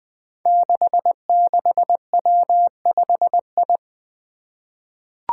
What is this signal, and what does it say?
Morse code, heard as a keyed tone, about 20 wpm, 700 Hz. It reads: 66W5I
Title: Five Challenging Letter-Number Combinations 20wpm